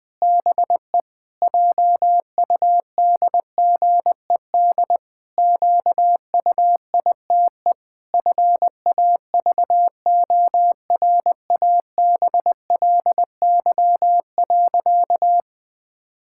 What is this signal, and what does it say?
Morse code, heard as a keyed tone, about 20 wpm, 700 Hz. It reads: BE JUDGED QUITE FAVORABLY.
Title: Pangram Challenge v2 - 20wpm